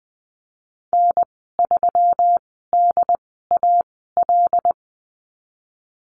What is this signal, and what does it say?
Morse code, heard as a keyed tone, about 20 wpm, spaced as for 15 wpm, 700 Hz. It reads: N3DAL
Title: 154 Lesson 25g - Letter p - Call Signs 20wpm